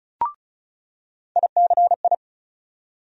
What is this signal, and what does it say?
Morse code, heard as a keyed tone, about 34 wpm, 700 Hz. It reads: ICI
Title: USA Call Sign Postfixes - 3 Letters 35wpm